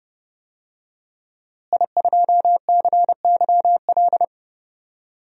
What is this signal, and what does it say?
Morse code, heard as a keyed tone, about 30 wpm, 700 Hz. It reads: I2CYL
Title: DXCC-100 Call Signs - 5 Characters - Repeated 3x - 30wpm